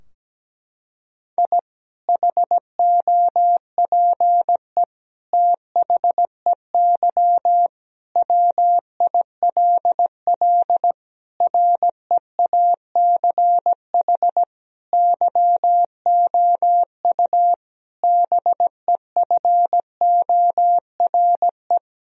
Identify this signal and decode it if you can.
Morse code, heard as a keyed tone, about 17 wpm, 700 Hz. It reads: I HOPE THEY WILL REACH YOU BEFORE
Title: Sentences from Top 1000 Words - Review Part 2 - 17wpm